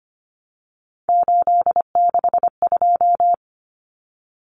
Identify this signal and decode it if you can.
Morse code, heard as a keyed tone, about 25 wpm, 700 Hz. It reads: 862
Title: Numbers - 3-Digit 25wpm